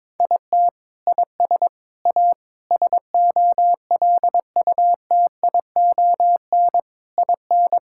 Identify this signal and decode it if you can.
Morse code, heard as a keyed tone, about 22 wpm, 700 Hz. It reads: IT IS A SOLUTION IN
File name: Sentences with 7-Word Phrases - 22wpm